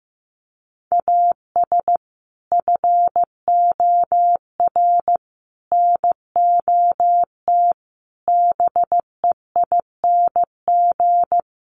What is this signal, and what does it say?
Morse code, heard as a keyed tone, about 15 wpm, 700 Hz. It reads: AS FOR NOT BEING
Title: Sentences from Top 1100 Words 15wpm